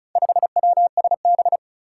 Morse code, heard as a keyed tone, about 36 wpm, 700 Hz. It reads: 5WSB
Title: Four Challenging Letter-Number Combinations 35wpm